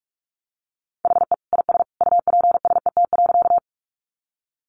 Morse code, heard as a keyed tone, about 44 wpm, 700 Hz. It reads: HE IS UPSET.
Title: Additional Easy Phrases 45wpm